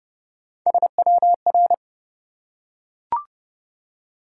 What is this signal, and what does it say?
Morse code, heard as a keyed tone, about 29 wpm, 700 Hz. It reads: SWR
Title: Top 180 QSO Elements 30wpm